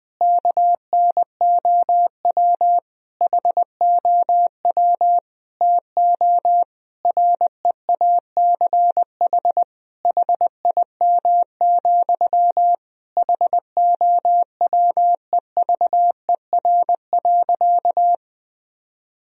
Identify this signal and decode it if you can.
Morse code, heard as a keyed tone, about 20 wpm, 700 Hz. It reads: KNOW HOW TO REACH HIM, HOWEVER.
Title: Sentences from Top 1000 Words - Review Part 2 - 20wpm